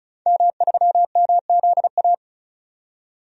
Morse code, heard as a keyed tone, about 36 wpm, 700 Hz. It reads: M3MZA
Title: DXCC-100 Call Signs - 5 Characters - 35wpm